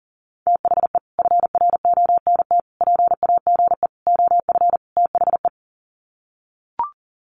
Morse code, heard as a keyed tone, about 40 wpm, 700 Hz. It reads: THE FRONT PAGE OF THE
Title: Sets of 5 Words Encore 40wpm